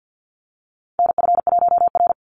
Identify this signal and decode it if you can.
Morse code, heard as a keyed tone, about 49 wpm, 700 Hz. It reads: NF1R
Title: NA SOTA Chaser Callsigns 50wpm